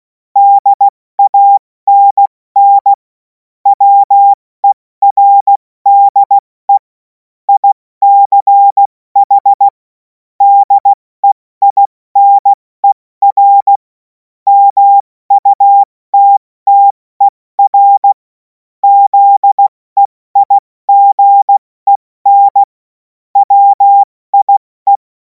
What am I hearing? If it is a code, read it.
Morse code, heard as a keyed tone, about 16 wpm, 800 Hz. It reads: DANN WERDE ICH DEINER MUTTER ZEIGEN WIE
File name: Morsekurs / Witze / Witz 24. März 2026 70 F80